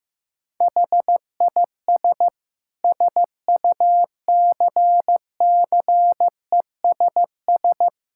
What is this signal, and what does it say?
Morse code, heard as a keyed tone, about 15 wpm, 700 Hz. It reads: HIS SUCCESS
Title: Sentences from Top 1000 Words - Review Part 6 - 15wpm